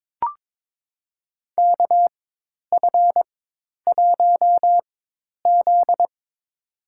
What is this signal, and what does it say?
Morse code, heard as a keyed tone, about 21 wpm, spaced as for 10 wpm, 700 Hz. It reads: KF1Z
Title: USA Call Signs - 4x Character Spacing - 22wpm